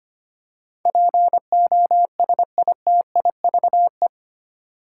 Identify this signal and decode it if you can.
Morse code, heard as a keyed tone, about 25 wpm, 700 Hz. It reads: POSITIVE
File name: Eight Letter Words - 25wpm